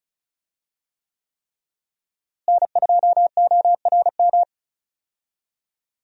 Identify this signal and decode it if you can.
Morse code, heard as a keyed tone, about 36 wpm, 700 Hz. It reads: N2ORM
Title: USA Call Signs - 5 Characters - Repeated 3x - 35wpm